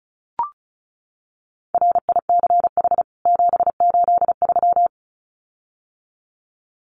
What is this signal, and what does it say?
Morse code, heard as a keyed tone, about 34 wpm, 700 Hz. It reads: RICH 783
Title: CWT Exchange 35wpm